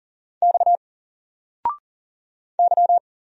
Morse code, heard as a keyed tone, about 40 wpm, 700 Hz. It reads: X Y
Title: Single Character (Letter:Number:Punctuation:Prosign) Alt 40wpm